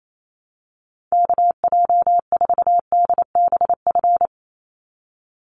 Morse code, heard as a keyed tone, about 28 wpm, 700 Hz. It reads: KJ4DBF
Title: USA Call Signs 28wpm